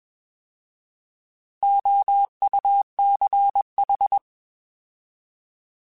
Morse code, heard as a keyed tone, about 21 wpm, 800 Hz. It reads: OUCH
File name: Alphabetical / Alphabetical mp3s / Os @20